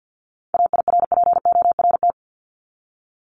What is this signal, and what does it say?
Morse code, heard as a keyed tone, about 50 wpm, 700 Hz. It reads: AIRPORT
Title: Top 1751-2000 Words 50wpm